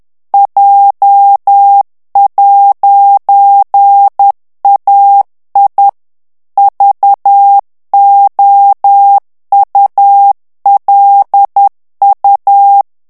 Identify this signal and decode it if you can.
Morse code, heard as a keyed tone, about 10 wpm, 800 Hz. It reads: J'AI VOULU
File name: morse mp3 / FRANCAIS TEXTE CLAIR 2 VITESSE 10 MOTS MINUTES